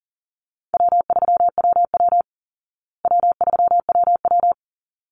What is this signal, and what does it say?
Morse code, heard as a keyed tone, about 40 wpm, 700 Hz. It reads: W3WW W3WW
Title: Call Signs Repeated 3x - 40wpm